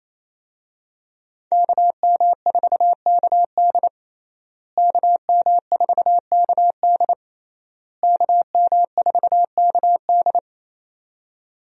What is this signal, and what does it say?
Morse code, heard as a keyed tone, about 27 wpm, 700 Hz. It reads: KM4KD KM4KD KM4KD
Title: USA Call Signs - 5 Characters - Repeated 3x - 28wpm